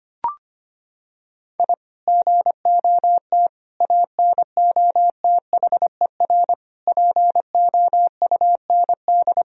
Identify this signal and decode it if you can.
Morse code, heard as a keyed tone, about 25 wpm, 700 Hz. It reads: I GOT ANOTHER POUND
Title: Sentences from Top 1000 Words - Review Part 2 - 25wpm